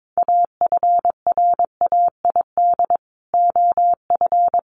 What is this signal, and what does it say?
Morse code, heard as a keyed tone, about 22 wpm, 700 Hz. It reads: AFRAID OF
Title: Sentences from Top 1500 Words 22wpm